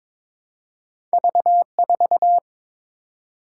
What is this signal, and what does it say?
Morse code, heard as a keyed tone, about 22 wpm, 700 Hz. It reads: V4
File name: practice / 22wpm / Two Letter-Number Combinations - 22wpm